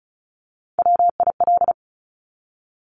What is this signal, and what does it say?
Morse code, heard as a keyed tone, about 36 wpm, 700 Hz. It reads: WIL
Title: Top Three Letter Combinations in Words 35wpm